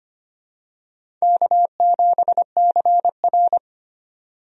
Morse code, heard as a keyed tone, about 25 wpm, 700 Hz. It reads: K7CR
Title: USA Call Signs - 4 Characters - 25wpm